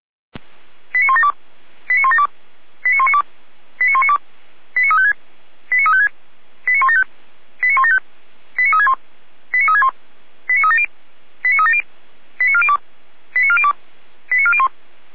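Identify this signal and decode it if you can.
Five-tone selective call (ZVEI), heard as quick runs of five tones, tones of about 80 ms. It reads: 89172 89172 89192 89192 89356 89356 89175 89175 89361 89361 89370 89370 89402 89402 89401
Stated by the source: Five-tone ZVEI selective call 89391 received by SDR, server of Feuerwehr Orlamünde.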